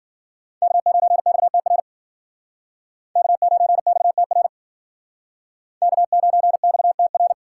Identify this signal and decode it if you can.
Morse code, heard as a keyed tone, about 54 wpm, 700 Hz. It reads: K9XTR K9XTR K9XTR